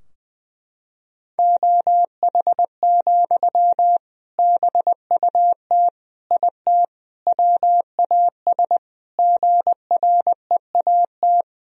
Morse code, heard as a keyed tone, about 20 wpm, 700 Hz. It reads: OH, BUT IT WAS GREAT
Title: Sentences from Top 1000 Words - Review Part 4 - 20wpm